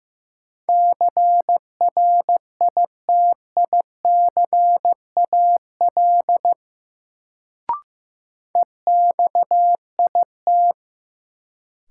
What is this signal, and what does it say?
Morse code, heard as a keyed tone, about 16 wpm, 700 Hz. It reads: CRITICAL EXIT
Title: Sentences from Top 1100 Words - New Word Drill Alt - 15wpm